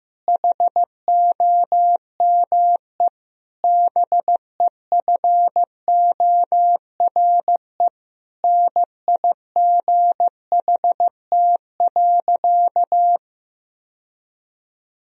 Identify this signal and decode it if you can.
Morse code, heard as a keyed tone, about 15 wpm, 700 Hz. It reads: HOME BEFORE NIGHT.
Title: Sentences from Top 1000 Words - Review Part 3 - 15wpm